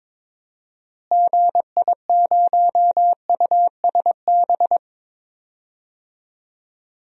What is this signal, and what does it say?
Morse code, heard as a keyed tone, about 22 wpm, 700 Hz. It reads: GI0USB
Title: DXCC-100 Call Signs - 6 Characters - Repeated 3x - 22wpm